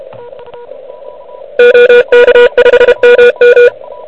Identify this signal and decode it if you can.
Morse code, heard as a keyed tone, about 29 wpm, 480 Hz. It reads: OK5MM
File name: ARCHIV / QCX / QCX Pileup